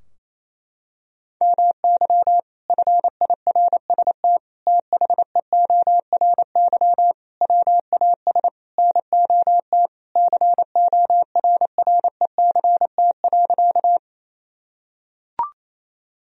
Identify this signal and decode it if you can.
Morse code, heard as a keyed tone, about 28 wpm, 700 Hz. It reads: MY FIRST THEORY WAS NOT CORRECT.
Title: Sentences from Top 1000 Words - Review Part 3 - 28wpm